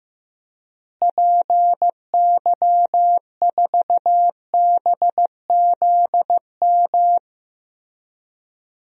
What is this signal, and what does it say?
Morse code, heard as a keyed tone, about 15 wpm, 700 Hz. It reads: PY4BZM
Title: DXCC-100 Call Signs - 6 Characters - Repeated 3x - 15wpm